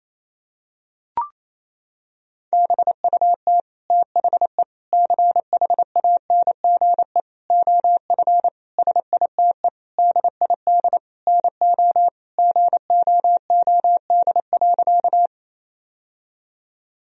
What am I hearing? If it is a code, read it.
Morse code, heard as a keyed tone, about 28 wpm, 700 Hz. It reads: BUT THE CHANGE OF SITE DID NO GOOD.
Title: Sentences from Top 1000 Words - Review Part 4 - 28wpm